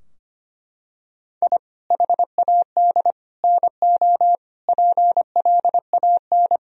Morse code, heard as a keyed tone, about 25 wpm, 700 Hz. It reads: I HAD NO PLAN
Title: Sentences from Top 1000 Words - Review Part 2 - 25wpm